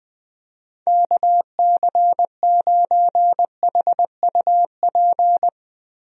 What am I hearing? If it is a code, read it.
Morse code, heard as a keyed tone, about 21 wpm, 700 Hz. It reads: KC9HUP
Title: USA Call Signs - 6 Characters - Repeated 3x - 20wpm